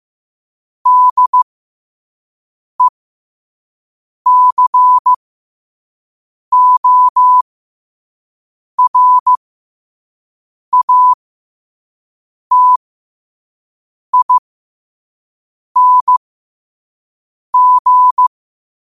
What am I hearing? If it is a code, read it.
Morse code, heard as a keyed tone, about 15 wpm, spaced as for 5 wpm, 1000 Hz. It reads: DECORATING
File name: qotdfiles / 5wpm